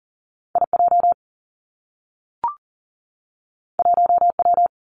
Morse code, heard as a keyed tone, about 42 wpm, 700 Hz. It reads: I J 1 W
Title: Two Challenging Letter-Number Combinations 40wpm